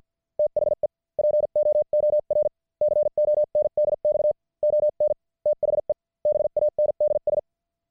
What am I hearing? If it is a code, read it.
Morse code, heard as a keyed tone, about 50 wpm, spaced as for 43 wpm, 600 Hz. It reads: THE POOR CONDX ON THE BANDS
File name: archived / CWBros